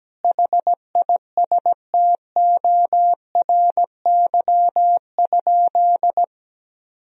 Morse code, heard as a keyed tone, about 17 wpm, 700 Hz. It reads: HISTORY?
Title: Sentences from Top 1000 Words - Review Part 4 - 17wpm